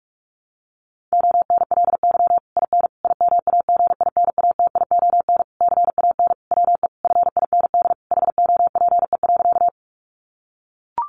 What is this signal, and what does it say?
Morse code, heard as a keyed tone, about 44 wpm, 700 Hz. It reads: ONLY IN IMAGINATION CAN WE FIND HOPE.
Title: Sentences from Top 1300 Words 45wpm